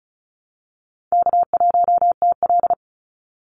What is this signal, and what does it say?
Morse code, heard as a keyed tone, about 36 wpm, 700 Hz. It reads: K1TL
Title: Call Signs - 35wpm